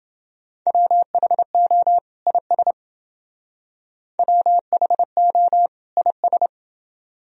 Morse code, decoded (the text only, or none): WHO IS WHO IS